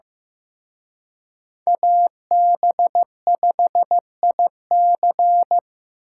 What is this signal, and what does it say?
Morse code, heard as a keyed tone, about 18 wpm, 700 Hz. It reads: AB5IC
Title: Speed Racing - USA 2x2 Call Signs 15wpm